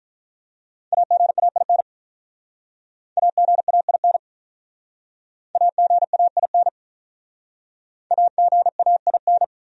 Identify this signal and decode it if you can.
Morse code, heard as a keyed tone, about 42 wpm, 700 Hz. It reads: AGAIN AGAIN AGAIN AGAIN
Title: Speed-Racing Top 300 Words 35wpm